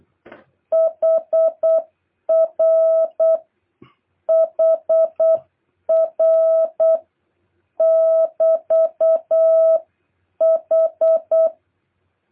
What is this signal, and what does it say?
Morse code, heard as a keyed tone, about 8 wpm, 650 Hz. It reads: HR HR <BT> H
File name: jh8cbh / 2023cw